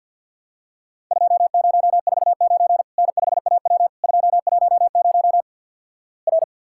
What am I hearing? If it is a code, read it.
Morse code, heard as a keyed tone, about 50 wpm, 700 Hz. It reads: 2049 N5AW 210 R
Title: Worked All Europe DX Contest - QTC Exchange - 50wpm